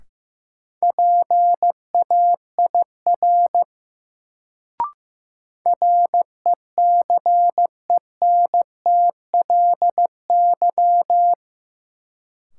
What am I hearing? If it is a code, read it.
Morse code, heard as a keyed tone, about 15 wpm, 700 Hz. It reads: PAIR RECENTLY
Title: Sentences from Top 1200 Words - New Word Drill Alt - 15wpm